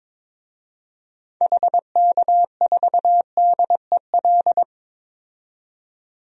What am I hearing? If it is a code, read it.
Morse code, heard as a keyed tone, about 22 wpm, 700 Hz. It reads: HK4DEL
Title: DXCC-100 Call Signs - 6 Characters - 22wpm